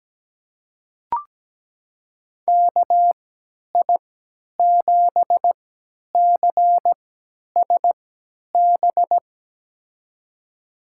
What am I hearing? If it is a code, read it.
Morse code, heard as a keyed tone, about 17 wpm, spaced as for 10 wpm, 700 Hz. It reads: KI7CSB
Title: USA Call Signs - 6 Characters - 3x Character Spacing - 17wpm